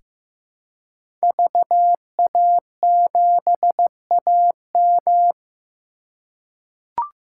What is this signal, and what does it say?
Morse code, heard as a keyed tone, about 15 wpm, 700 Hz. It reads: VA7AM
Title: K1USN SST QSO Practice - 15wpm